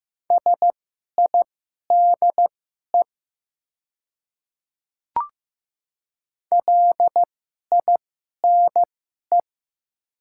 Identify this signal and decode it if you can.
Morse code, heard as a keyed tone, about 15 wpm, spaced as for 11 wpm, 690 Hz. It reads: SIDE LINE
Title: 2x Character Spacing - Top 200 Words - 15wpm